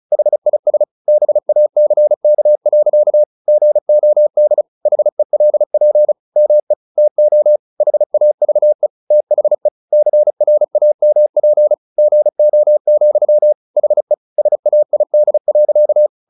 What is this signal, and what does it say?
Morse code, heard as a keyed tone, about 36 wpm, 600 Hz. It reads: HIS BACK. GOD HELP ME TO HAVE THE CRAMP GO, HE SAID.
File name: the old man and the sea / the old man and the sea 1 35wpm